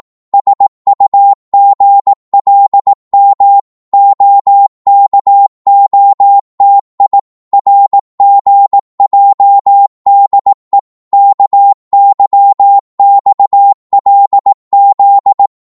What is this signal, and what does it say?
Morse code, heard as a keyed tone, about 18 wpm, 800 Hz. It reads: SUGLM OKOTI RGJDE KYXLZ
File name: Morse / ltr-18wpm